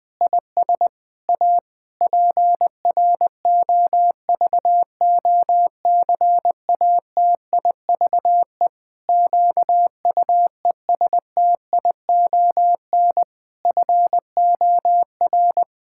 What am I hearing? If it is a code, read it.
Morse code, heard as a keyed tone, about 20 wpm, 700 Hz. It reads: IS A PROVOCATIVE QUESTION FOR
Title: Pangram Challenge v2 - 20wpm